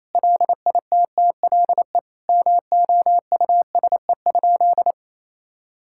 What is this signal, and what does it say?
Morse code, heard as a keyed tone, about 28 wpm, 700 Hz. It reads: LITTLE MOUSE?
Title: Sentences from Top 1000 Words - Review Part 5 - 28wpm